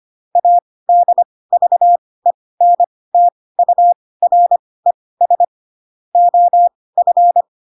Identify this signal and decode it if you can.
Morse code, heard as a keyed tone, about 25 wpm, spaced as for 18 wpm, 700 Hz. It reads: ADVENTURES OF